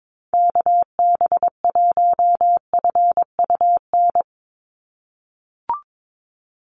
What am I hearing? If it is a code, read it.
Morse code, heard as a keyed tone, about 22 wpm, 700 Hz. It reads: KB1FUN
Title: USA Call Signs 22wpm